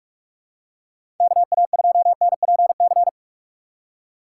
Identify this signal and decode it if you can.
Morse code, heard as a keyed tone, about 44 wpm, 700 Hz. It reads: KA2NPC
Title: USA Call Signs - 6 Characters - 6x Character Spacing - 45wpm